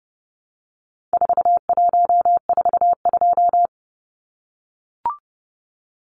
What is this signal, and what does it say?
Morse code, heard as a keyed tone, about 30 wpm, 700 Hz. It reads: 4142
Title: Numbers - 4-Digit 30wpm